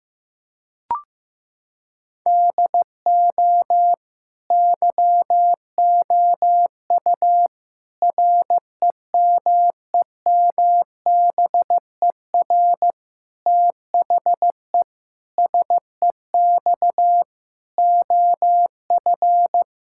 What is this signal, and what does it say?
Morse code, heard as a keyed tone, about 14 wpm, 700 Hz. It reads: DO YOU REMEMBER THE SEX OF
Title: Sentences from Top 1600 Words 15wpm